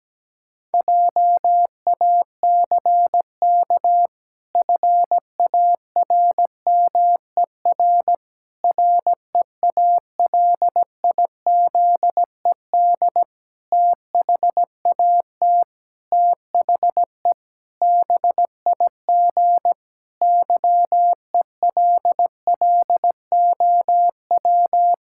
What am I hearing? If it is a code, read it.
Morse code, heard as a keyed tone, about 17 wpm, 700 Hz. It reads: JACK FARMER REALIZED THAT THE BIG YELLOW
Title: Pangram Challenge v2 - 17wpm